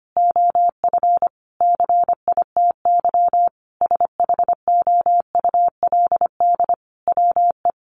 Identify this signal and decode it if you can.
Morse code, heard as a keyed tone, about 25 wpm, 700 Hz. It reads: OF CITY SHOULD WE
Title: Sentences from Top 1300 Words 25wpm